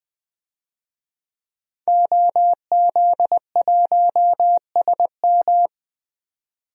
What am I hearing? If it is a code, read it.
Morse code, heard as a keyed tone, about 21 wpm, 700 Hz. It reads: OZ1SM
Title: DXCC-100 Call Signs - 5 Characters - 20wpm